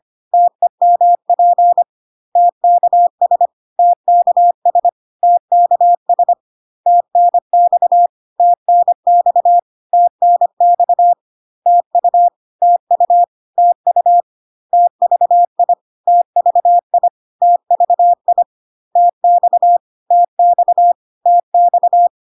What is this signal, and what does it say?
Morse code, heard as a keyed tone, about 25 wpm, 700 Hz. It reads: TEMP TKS TKS TKS TNX TNX TNX TU TU TU TVI TVI TVI TX TX TX
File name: cw / CW common abbreviations alphabetical repeated 3 times 25 25wpm ik1hge